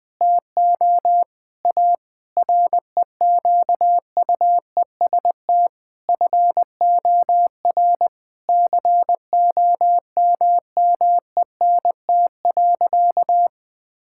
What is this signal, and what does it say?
Morse code, heard as a keyed tone, about 21 wpm, 700 Hz. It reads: TO A REQUEST FOR COMMENT.
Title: Sentences with 8-Word Phrases - 20wpm